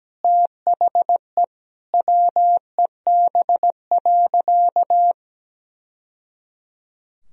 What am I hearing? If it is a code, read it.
Morse code, heard as a keyed tone, about 17 wpm, 700 Hz. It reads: THE WEB.
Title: Sentences from Top 1000 Words - Review Part 4 - 17wpm